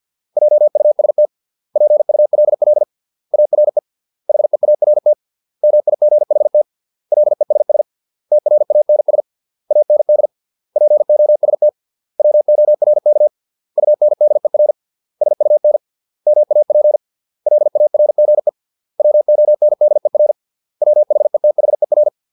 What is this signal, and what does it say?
Morse code, heard as a keyed tone, about 50 wpm, 600 Hz. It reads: JUST PULL ARE HEART MIGHT LESS TRANS AND POST WORK UNDER SUN MAP LARGE WONDER WHETHER